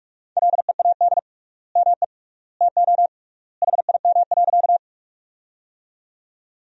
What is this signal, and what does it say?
Morse code, heard as a keyed tone, about 47 wpm, 700 Hz. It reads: LEAD ME TO HIM.
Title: 3x Word-Spacing - Sentences from Top 500 Words - 45wpm